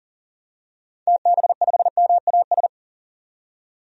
Morse code, heard as a keyed tone, about 40 wpm, 700 Hz. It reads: TB5MAS
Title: DXCC-100 Call Signs - 6 Characters - 40wpm